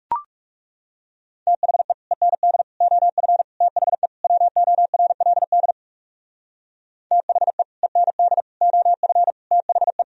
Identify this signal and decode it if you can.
Morse code, heard as a keyed tone, about 36 wpm, 700 Hz. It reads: THE END OF THE WORLD THE END OF THE
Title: Speed-Racing - Sets of 5 Words 30wpm